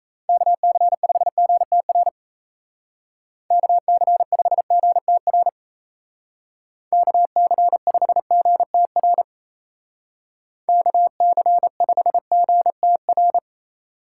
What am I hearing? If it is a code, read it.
Morse code, heard as a keyed tone, about 36 wpm, 700 Hz. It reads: KC5GTR KC5GTR KC5GTR KC5GTR